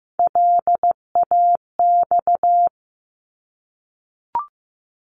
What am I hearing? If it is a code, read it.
Morse code, heard as a keyed tone, about 15 wpm, 700 Hz. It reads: LAX
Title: ARRL Sections 15wpm